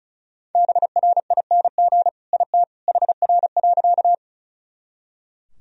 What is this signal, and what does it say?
Morse code, heard as a keyed tone, about 34 wpm, 700 Hz. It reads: BRING IT HR.
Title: Sentences from Top 500 Words and Abbreviations 35wpm